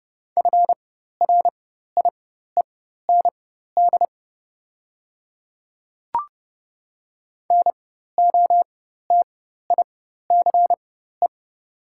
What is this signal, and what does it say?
Morse code, heard as a keyed tone, about 30 wpm, spaced as for 14 wpm, 690 Hz. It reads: FRIEND NOTICE
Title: 4x Character Spacing - Top 500 Words - 30wpm